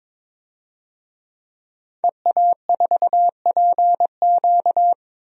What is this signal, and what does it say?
Morse code, heard as a keyed tone, about 22 wpm, 700 Hz. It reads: EA4PQ
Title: DXCC-100 Call Signs - 5 Characters - Repeated 2x - 22wpm